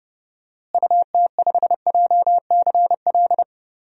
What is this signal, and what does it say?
Morse code, heard as a keyed tone, about 30 wpm, 700 Hz. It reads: UT5JCL
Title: DXCC-100 Call Signs - 6 Characters - Repeated 2x - 30wpm